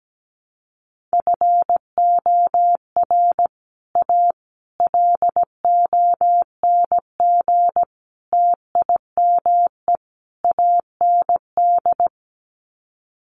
Morse code, heard as a keyed tone, about 17 wpm, 700 Hz. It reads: FOR A LONG TIME AND
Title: Sets of 5 Words Encore2 17wpm